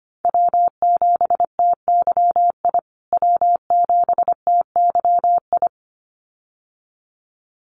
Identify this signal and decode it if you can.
Morse code, heard as a keyed tone, about 25 wpm, 700 Hz. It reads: W7TYI W7TYI
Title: USA Call Signs Repeated 2x 25wpm